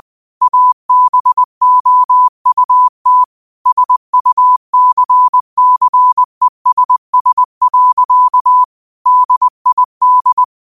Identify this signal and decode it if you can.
Morse code, heard as a keyed tone, about 20 wpm, 1000 Hz. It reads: ABOUT SUCCESS. DID
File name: qotdfiles / 20wpm